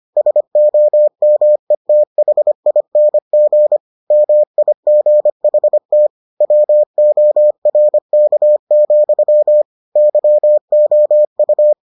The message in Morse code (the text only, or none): SOMETHING MIGHT WORK, YOU